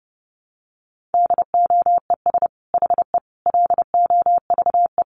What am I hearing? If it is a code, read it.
Morse code, heard as a keyed tone, about 30 wpm, 700 Hz. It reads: DOES HE LOVE
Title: Sentences from Top 400 Words 30wpm